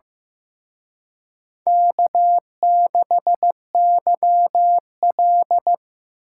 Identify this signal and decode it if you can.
Morse code, heard as a keyed tone, about 18 wpm, 700 Hz. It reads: K6YL
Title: Speed Racing - USA 1x2 Call Signs 15wpm